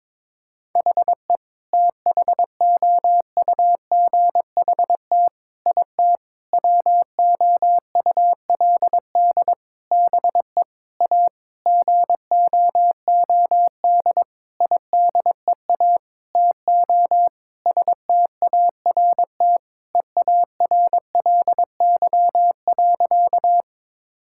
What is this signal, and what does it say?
Morse code, heard as a keyed tone, about 22 wpm, 700 Hz. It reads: HE THOUGHT IT WOULD BE A GOOD IDEA TO START EARLY.
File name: practice / 22wpm / Sentences with 8-Word Phrases - 22wpm